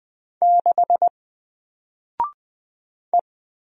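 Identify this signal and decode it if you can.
Morse code, heard as a keyed tone, about 20 wpm, 700 Hz. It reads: 6 E
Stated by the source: Single letter and number Morse drill at 20 wpm, from N4LCD.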